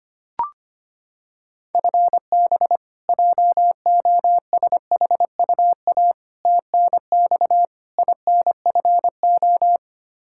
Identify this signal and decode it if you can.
Morse code, heard as a keyed tone, about 25 wpm, 700 Hz. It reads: FB JOSHUA TNX INFO
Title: K1USN SST QSO Practice 25wpm